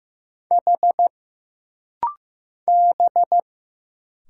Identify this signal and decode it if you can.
Morse code, heard as a keyed tone, about 15 wpm, 700 Hz. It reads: H B
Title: Single Letter-Number Alt - Rapid-Fire 15wpm